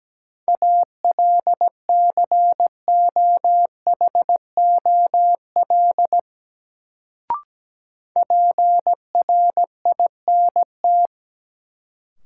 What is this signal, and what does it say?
Morse code, heard as a keyed tone, about 17 wpm, 700 Hz. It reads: ALCOHOL PRINT
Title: Sentences from Top 1400 Words - New Word Drill Alt - 17wpm